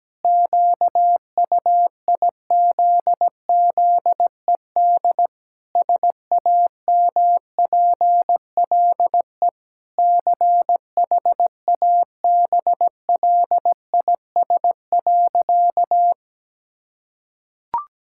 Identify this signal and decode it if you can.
Morse code, heard as a keyed tone, about 17 wpm, 700 Hz. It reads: QUIZZED SAMPLE CHABLIS.
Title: Pangram Challenge v2 - 17wpm